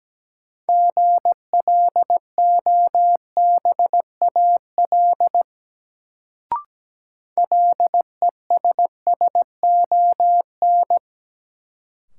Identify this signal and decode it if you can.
Morse code, heard as a keyed tone, about 17 wpm, 700 Hz. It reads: GLOBAL LESSON
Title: Sentences from Top 1200 Words - New Word Drill Alt - 17wpm